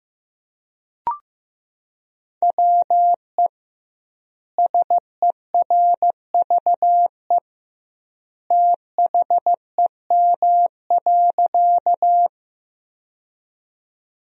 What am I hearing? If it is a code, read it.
Morse code, heard as a keyed tone, about 15 wpm, 700 Hz. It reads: WE SERVE THEM.
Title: 2x Word-Spacing - Sentences from Top 400 Words - 15wpm